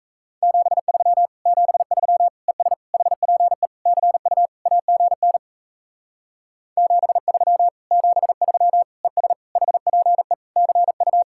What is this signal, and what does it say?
Morse code, heard as a keyed tone, about 36 wpm, 700 Hz. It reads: 73 73 ES HPE CU AGN 73 73 ES HPE CU